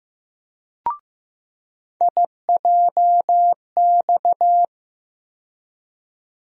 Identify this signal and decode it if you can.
Morse code, heard as a keyed tone, about 15 wpm, 700 Hz. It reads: IJX
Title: USA Call Sign Postfixes - 3 Letters 15wpm